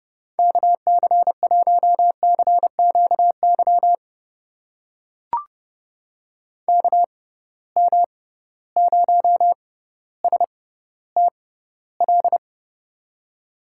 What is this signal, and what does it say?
Morse code, heard as a keyed tone, about 30 wpm, spaced as for 15 wpm, 700 Hz. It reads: KC1CQY KM0STL
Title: USA Call Signs - 6 Characters - 6x Character Spacing - 30wpm